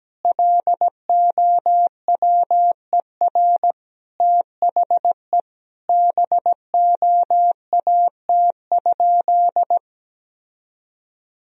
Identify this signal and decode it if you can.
Morse code, heard as a keyed tone, about 17 wpm, 700 Hz. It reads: LOWER THE BOAT?
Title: Sentences from Top 1000 Words - Review Part 4 - 17wpm